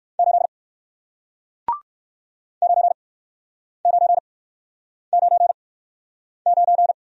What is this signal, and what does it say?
Morse code, heard as a keyed tone, about 56 wpm, spaced as for 9 wpm, 700 Hz. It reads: 8 9999
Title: Speed-Racing Single Letter-Number 45wpm